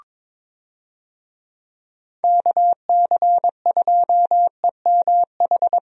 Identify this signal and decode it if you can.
Morse code, heard as a keyed tone, about 21 wpm, 700 Hz. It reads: KC2EMH
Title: USA Call Signs - 6 Characters - 22wpm